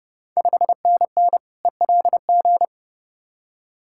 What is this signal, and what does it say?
Morse code, heard as a keyed tone, about 32 wpm, 700 Hz. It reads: 5NN ELG
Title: Ontario QSO Party Exchange 30wpm